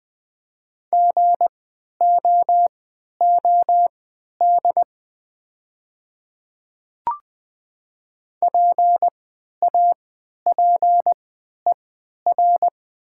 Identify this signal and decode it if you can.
Morse code, heard as a keyed tone, about 20 wpm, spaced as for 11 wpm, 700 Hz. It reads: GOOD PAPER
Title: 3x Character Spacing - Top 300 Words - 20wpm